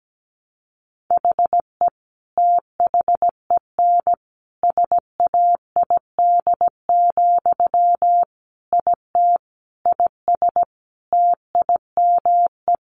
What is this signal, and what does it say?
Morse code, heard as a keyed tone, about 17 wpm, 700 Hz. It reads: HE THEN SAID, IT IS TIME
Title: Sentences from Top 300 Words 17wpm